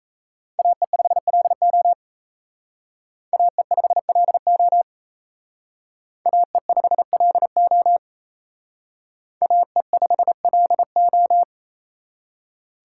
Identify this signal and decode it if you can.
Morse code, heard as a keyed tone, about 35 wpm, 700 Hz. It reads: AE5LO AE5LO AE5LO AE5LO
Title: Speed Racing - USA 2x2 Call Signs 28wpm